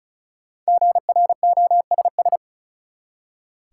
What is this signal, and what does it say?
Morse code, heard as a keyed tone, about 34 wpm, 700 Hz. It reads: GROSS
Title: Five Letter Words - 35wpm